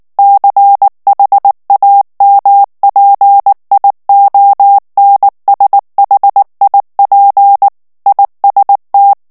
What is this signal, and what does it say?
Morse code, heard as a keyed tone, about 19 wpm, 800 Hz. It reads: CHAMPIONSHIP IST
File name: morse mp3 / ALLEMAND texte clair 2 vitesse 18 mots minute